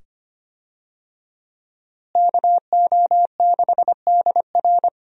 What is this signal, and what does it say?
Morse code, heard as a keyed tone, about 25 wpm, 700 Hz. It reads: KO6DR
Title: USA Call Signs - 5 Characters - Repeated 2x - 25wpm